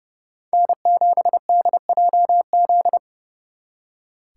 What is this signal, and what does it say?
Morse code, heard as a keyed tone, about 36 wpm, 700 Hz. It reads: N7DJZ
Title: Speed Racing - USA 1x3 Call Signs 30wpm